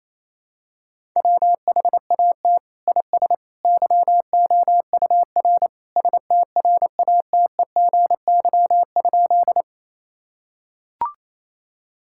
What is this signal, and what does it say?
Morse code, heard as a keyed tone, about 28 wpm, 700 Hz. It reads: WHAT IS YOUR STRATEGY?
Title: Sentences from Top 1000 Words - Review Part 5 - 28wpm